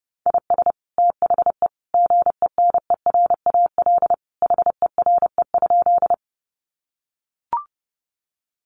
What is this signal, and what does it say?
Morse code, heard as a keyed tone, about 30 wpm, 700 Hz. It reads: IS THE GENERAL HERE?
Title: Sentences from Top 600 Words 30wpm